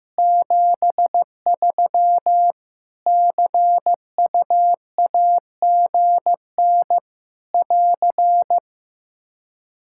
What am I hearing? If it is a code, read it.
Morse code, heard as a keyed tone, about 15 wpm, 700 Hz. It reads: 73 CUAGN <AR>
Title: Practice QSO - Exchange 4 - Ending - 15wpm